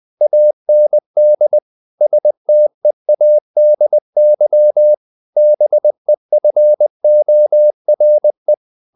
Morse code, heard as a keyed tone, about 20 wpm, 600 Hz. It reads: AND STEADY BEFORE
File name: the old man and the sea part 2 20wpm